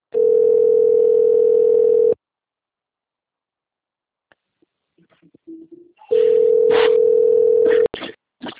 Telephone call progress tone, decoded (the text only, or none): ringback tone